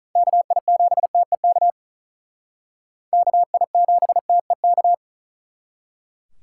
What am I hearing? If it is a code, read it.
Morse code, heard as a keyed tone, about 44 wpm, 700 Hz. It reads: KI7TEK KI7TEK